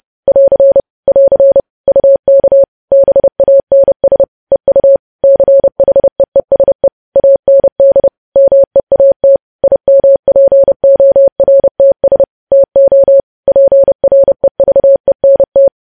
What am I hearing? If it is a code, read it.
Morse code, heard as a keyed tone, about 30 wpm, 550 Hz. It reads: <AR> <AR> UK BANS EU CHEESE AND MEAT IMPORTS TO PREVENT
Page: news headlines in morse code at 30 wpm